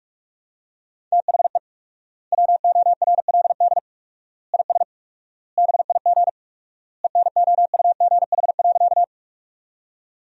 Morse code, heard as a keyed tone, about 43 wpm, 700 Hz. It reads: THE WORLD IS BIG ENOUGH.
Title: 4x Word-Spacing - Sentences from Top 400 Words - 45wpm